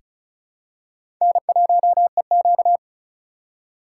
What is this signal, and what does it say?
Morse code, heard as a keyed tone, about 34 wpm, 700 Hz. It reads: N1EQ
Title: USA Call Signs - 4 Characters - 3x Character Spacing - 35wpm